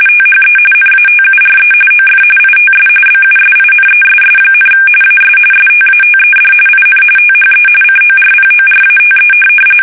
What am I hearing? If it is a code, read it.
RTTY, about 45.45 baud. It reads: THE NATURE OF ITS BUSINESS, AS STATED IN THE CORPORATE
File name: fsk850 / teletype-corp